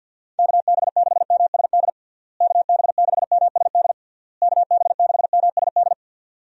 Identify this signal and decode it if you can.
Morse code, heard as a keyed tone, about 46 wpm, 700 Hz. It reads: KB6MSD KB6MSD KB6MSD
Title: USA Call Signs - 6 Characters - Repeated 3x - 50wpm